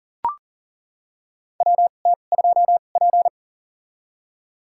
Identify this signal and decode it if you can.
Morse code, heard as a keyed tone, about 40 wpm, spaced as for 29 wpm, 700 Hz. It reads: WT2P
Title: USA Call Signs - 4 Characters - 2x Character Spacing - 40wpm